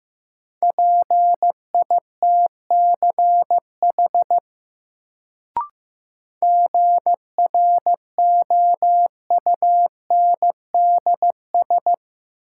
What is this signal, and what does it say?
Morse code, heard as a keyed tone, about 15 wpm, 700 Hz. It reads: PITCH GROUNDS
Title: Sentences from Top 1300 Words - New Word Drill Alt - 15wpm